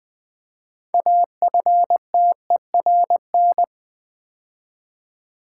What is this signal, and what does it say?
Morse code, heard as a keyed tone, about 20 wpm, 700 Hz. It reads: AFTERN
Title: Build A Word - 3 Letters At A Time - 9-Letter Words - 20wpm